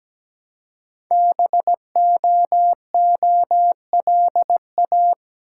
Morse code, heard as a keyed tone, about 17 wpm, 700 Hz. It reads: BOOLA
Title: Code Groups - 5 Letters - 17wpm